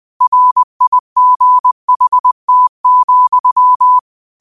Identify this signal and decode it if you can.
Morse code, heard as a keyed tone, about 20 wpm, 1000 Hz. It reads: RIGHT,